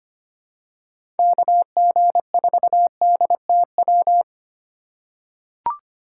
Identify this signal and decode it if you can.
Morse code, heard as a keyed tone, about 25 wpm, 700 Hz. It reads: KG4DTW
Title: USA Call Signs - 6 Characters - 2x Character Spacing - 25wpm